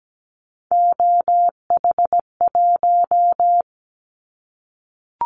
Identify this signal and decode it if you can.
Morse code, heard as a keyed tone, about 17 wpm, 700 Hz. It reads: OH1
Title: Callsign Prefixes for Top 50 DXCC - 17wpm